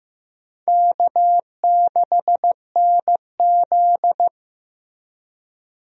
Morse code, heard as a keyed tone, about 15 wpm, 700 Hz. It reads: K6NZ
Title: USA Call Signs - 4 Characters - Repeated 3x - 15wpm